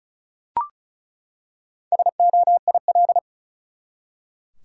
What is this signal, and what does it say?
Morse code, heard as a keyed tone, about 36 wpm, 700 Hz. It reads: SOIL
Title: Four Letter Words - 35wpm